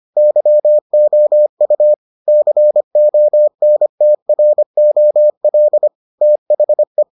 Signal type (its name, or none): Morse code